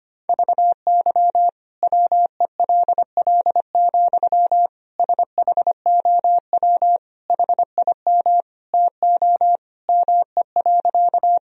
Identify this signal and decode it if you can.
Morse code, heard as a keyed tone, about 25 wpm, 700 Hz. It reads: VY WELL, SHOW HIM TO ME.
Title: Sentences from Top 200 Words and Abbreviations 25wpm